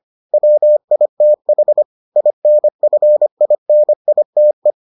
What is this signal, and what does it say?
Morse code, heard as a keyed tone, about 25 wpm, 600 Hz. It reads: WITH INFINITE